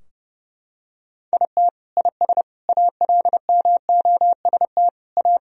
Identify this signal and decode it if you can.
Morse code, heard as a keyed tone, about 32 wpm, 700 Hz. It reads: IT IS ALMOST A
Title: Sentences from Top 1000 Words - Review Part 4 - 30wpm